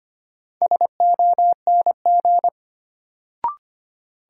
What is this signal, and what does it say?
Morse code, heard as a keyed tone, about 25 wpm, 700 Hz. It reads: SONG
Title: Sentences from Top 400 Words - New Word Drill - 25wpm